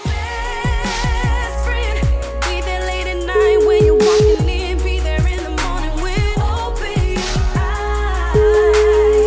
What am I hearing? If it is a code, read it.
Ringback tone (Europe, 425 Hz), with noise.